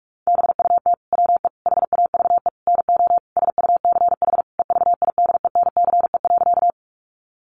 Morse code, heard as a keyed tone, about 44 wpm, 700 Hz. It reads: BUT WE HAVE NO SUCH EVIDENCE.
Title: Sentences from Top 1300 Words 45wpm